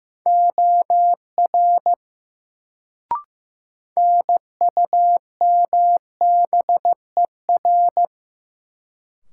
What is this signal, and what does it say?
Morse code, heard as a keyed tone, about 15 wpm, 700 Hz. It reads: OR NUMBER
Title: Sentences from Top 100 Words - New Word Drill - 15wpm